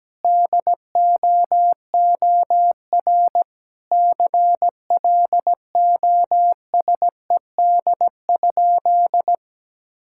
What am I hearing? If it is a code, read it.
Morse code, heard as a keyed tone, about 17 wpm, 700 Hz. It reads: DOOR CLOSED?
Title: Sentences from Top 1600 Words 17wpm